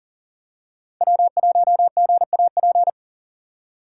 Morse code, heard as a keyed tone, about 40 wpm, 700 Hz. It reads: W1GAP
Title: USA Call Signs - 2x Character Spacing - 40wpm